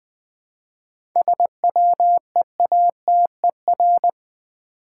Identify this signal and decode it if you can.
Morse code, heard as a keyed tone, about 20 wpm, 700 Hz. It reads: SWEATER
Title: Easy Words with Hints - 20wpm